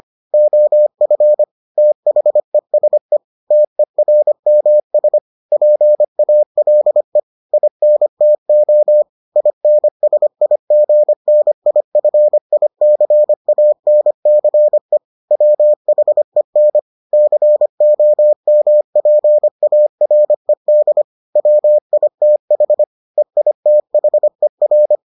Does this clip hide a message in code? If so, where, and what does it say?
Yes: Morse code from 0.3 to 24.9 s, heard as a keyed tone, about 25 wpm, 600 Hz. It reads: OF THESE TERMS PALE INTO INSIGNIFICANCE WHEN COMPARED WITH EITHER